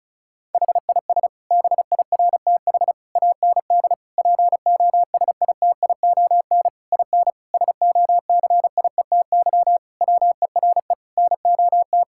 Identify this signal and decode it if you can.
Morse code, heard as a keyed tone, about 36 wpm, 700 Hz. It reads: HIS BIRTH AND POSITION IN SOCIETY WERE NOT
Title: Sentences from Top 1000 Words - Review Part 5 - 35wpm